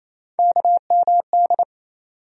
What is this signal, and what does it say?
Morse code, heard as a keyed tone, about 28 wpm, 700 Hz. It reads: KMD
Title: USA Call Sign Postfixes - 3 Letters 28wpm